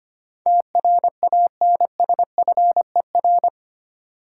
Morse code, heard as a keyed tone, about 25 wpm, 700 Hz. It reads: TRANSFER
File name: Eight Letter Words - 25wpm